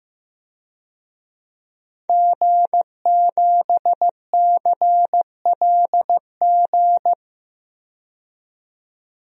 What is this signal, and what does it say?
Morse code, heard as a keyed tone, about 15 wpm, 700 Hz. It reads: G7CLG
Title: DXCC-100 Call Signs - 5 Characters - 15wpm